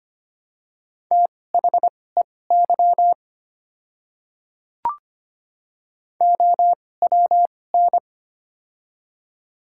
Morse code, heard as a keyed tone, about 25 wpm, spaced as for 18 wpm, 700 Hz. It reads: THEY OWN
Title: 2x Character Spacing - Top 300 Words - 25wpm